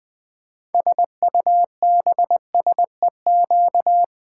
Morse code, heard as a keyed tone, about 20 wpm, 700 Hz. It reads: SUBSEQ
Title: Build A Word - 6 Letters At A Time - 12-Letter Words - 20wpm